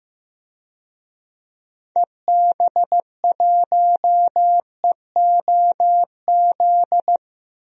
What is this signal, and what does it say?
Morse code, heard as a keyed tone, about 15 wpm, 700 Hz. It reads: EB1EOZ
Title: DXCC-100 Call Signs - 6 Characters - 15wpm